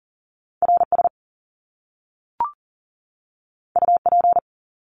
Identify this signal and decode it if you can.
Morse code, heard as a keyed tone, about 38 wpm, 700 Hz. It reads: RS UP
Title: Top Two Letter Combinations in Words 40wpm